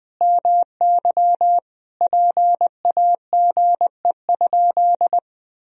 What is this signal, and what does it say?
Morse code, heard as a keyed tone, about 20 wpm, 700 Hz. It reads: MY PAGE?
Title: Sentences from Top 1000 Words - Review Part 1 - 20wpm